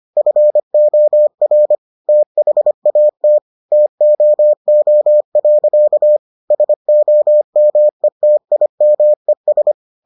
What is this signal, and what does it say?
Morse code, heard as a keyed tone, about 25 wpm, 600 Hz. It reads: FOR THAT TOO. SOMETIMES